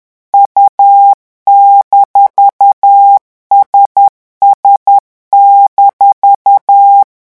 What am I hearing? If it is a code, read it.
Morse code, heard as a keyed tone, about 10 wpm, 800 Hz. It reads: U-SS-
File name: morse mp3 / revision17-vitesse 10 mots